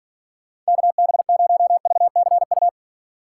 Morse code, heard as a keyed tone, about 50 wpm, 700 Hz. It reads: KB0VCU